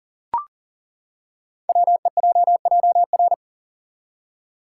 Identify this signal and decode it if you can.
Morse code, heard as a keyed tone, about 38 wpm, 700 Hz. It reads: WEJJR
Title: Code Groups - 5 Letters - 40wpm